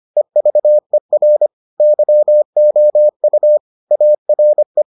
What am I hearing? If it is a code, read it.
Morse code, heard as a keyed tone, about 25 wpm, 600 Hz. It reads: EVER YOU ARE